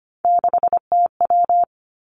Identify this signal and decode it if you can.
Morse code, heard as a keyed tone, about 25 wpm, 700 Hz. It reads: 6TW
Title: Three Challenging Letter-Number Combinations 25wpm